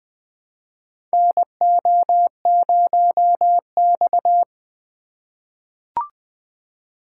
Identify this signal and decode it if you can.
Morse code, heard as a keyed tone, about 20 wpm, 700 Hz. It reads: NO0X
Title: USA Call Signs - 4 Characters - 3x Character Spacing - 20wpm